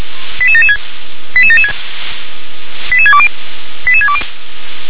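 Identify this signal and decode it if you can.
Five-tone selective call (ZVEI), heard as quick runs of five tones, tones of about 70 ms. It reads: 88776 88776 88620 88620
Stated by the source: Five-tone ZVEI selective call 88774, received on SDR.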